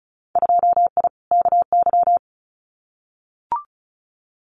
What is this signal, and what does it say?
Morse code, heard as a keyed tone, about 36 wpm, 700 Hz. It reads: 2I KY
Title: Winter Field Day Exchanges 35wpm